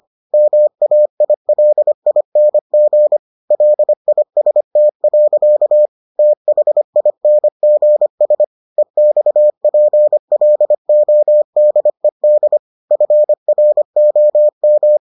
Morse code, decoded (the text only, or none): MAILING LIST. THINGS EXPLODED FROM